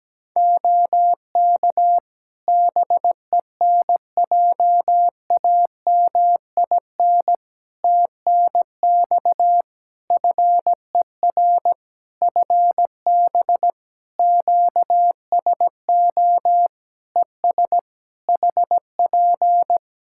Morse code, heard as a keyed tone, about 17 wpm, 700 Hz. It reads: OK BENJAMIN TNX FER FB QSO ES HP
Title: K1USN SST QSO Practice 17wpm